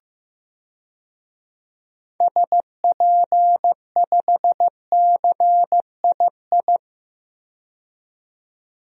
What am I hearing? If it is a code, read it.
Morse code, heard as a keyed tone, about 15 wpm, 700 Hz. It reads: SP5CII
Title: DXCC-100 Call Signs - 6 Characters - 15wpm